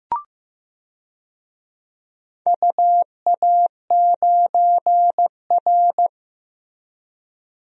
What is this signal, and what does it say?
Morse code, heard as a keyed tone, about 15 wpm, 700 Hz. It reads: UA9R
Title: DXCC-100 Call Signs - 4 Characters - Repeated 3x - 15wpm